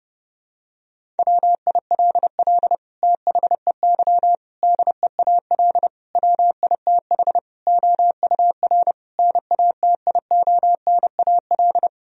Morse code, heard as a keyed tone, about 30 wpm, 700 Hz. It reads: WILL THEY DEAL WITH OUR NATIONAL
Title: Sentences from Top 1000 Words - Review Part 4 - 30wpm